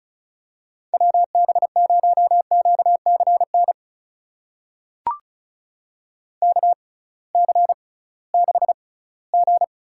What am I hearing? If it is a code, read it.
Morse code, heard as a keyed tone, about 36 wpm, spaced as for 20 wpm, 700 Hz. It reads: WB0QCN KC6G
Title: USA Call Signs - 6 Characters - 6x Character Spacing - 35wpm